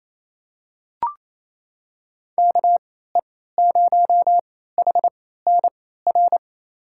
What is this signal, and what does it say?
Morse code, heard as a keyed tone, about 28 wpm, spaced as for 16 wpm, 700 Hz. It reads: KE0HNR
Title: USA Call Signs - 3x Character Spacing - 28wpm